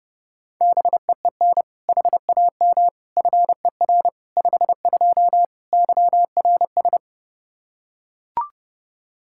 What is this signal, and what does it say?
Morse code, heard as a keyed tone, about 32 wpm, 700 Hz. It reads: BEEN HAM FER 52 YRS
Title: Practice QSO - Exchange 3 - More Info - 30wpm